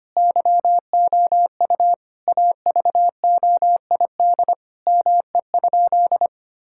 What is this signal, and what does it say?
Morse code, heard as a keyed tone, about 25 wpm, 700 Hz. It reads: YOU AVOID ME?
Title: Sentences from Top 1000 Words - Review Part 6 - 25wpm